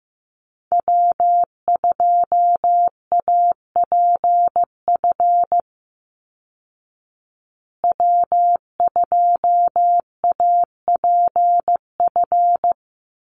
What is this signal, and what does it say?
Morse code, heard as a keyed tone, about 15 wpm, 700 Hz. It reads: W2APF W2APF
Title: Call Signs Repeated 2x - 15wpm